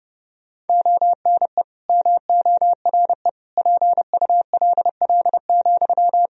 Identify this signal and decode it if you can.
Morse code, heard as a keyed tone, about 30 wpm, 700 Hz. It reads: ONE MORE PULL,
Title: Sentences from Top 1000 Words - Review Part 2 - 30wpm